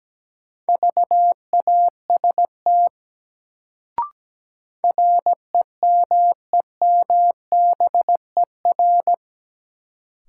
Morse code, heard as a keyed tone, about 17 wpm, 700 Hz. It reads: VAST REMEMBER
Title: Sets of 4 Words - New Word Drill - 17wpm